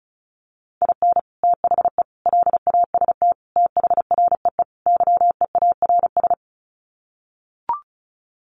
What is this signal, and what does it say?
Morse code, heard as a keyed tone, about 34 wpm, 700 Hz. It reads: IN THE LAST THREE YEARS
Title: Sets of 5 Words Encore2 35wpm